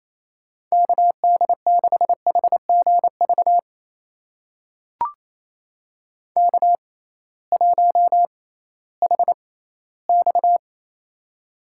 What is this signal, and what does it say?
Morse code, heard as a keyed tone, about 28 wpm, spaced as for 16 wpm, 700 Hz. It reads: KD6HGV K1HX
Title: USA Call Signs - 6x Character Spacing - 28wpm